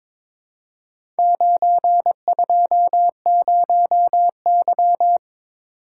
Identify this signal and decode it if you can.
Morse code, heard as a keyed tone, about 22 wpm, 700 Hz. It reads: 920Y
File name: practice / 22wpm / Four Letter-Number Combinations - 22wpm